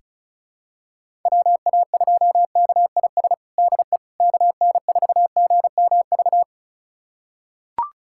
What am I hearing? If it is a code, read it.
Morse code, heard as a keyed tone, about 34 wpm, 700 Hz. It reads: WA2KIS DE KN4GMV
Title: USA Callsign DE Callsign 35wpm